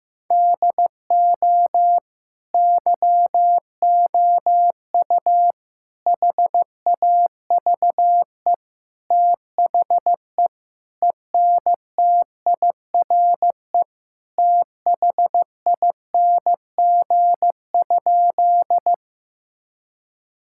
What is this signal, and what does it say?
Morse code, heard as a keyed tone, about 16 wpm, 700 Hz. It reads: DO YOU HAVE THE ENTIRE THING?
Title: Sentences from Top 1000 Words - Review Part 5 - 15wpm